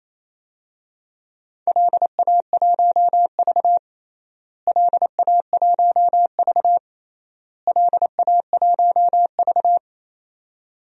Morse code, heard as a keyed tone, about 28 wpm, 700 Hz. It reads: LA1V LA1V LA1V
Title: DXCC-100 Call Signs - 4 Characters - Repeated 3x - 28wpm